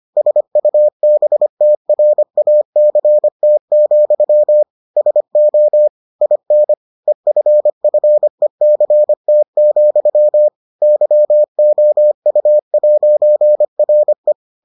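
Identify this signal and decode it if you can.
Morse code, heard as a keyed tone, about 25 wpm, 600 Hz. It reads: SUBTRACT, SO IN EFFECT, YOU'RE
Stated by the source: Morse edition of the Foundations of Amateur Radio episodes.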